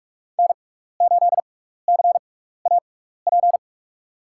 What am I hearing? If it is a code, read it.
Morse code, heard as a keyed tone, about 46 wpm, spaced as for 16 wpm, 700 Hz. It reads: N8CAP
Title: USA Call Signs - 5 Characters - 6x Character Spacing - 45wpm